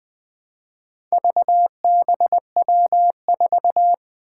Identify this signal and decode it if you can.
Morse code, heard as a keyed tone, about 20 wpm, 700 Hz. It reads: VBW4
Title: Four Challenging Letter-Number Combinations 20wpm